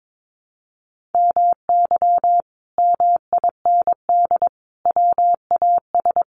Morse code, heard as a keyed tone, about 22 wpm, 700 Hz. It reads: MY MIND WAS
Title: Sentences from Top 500 Words 22wpm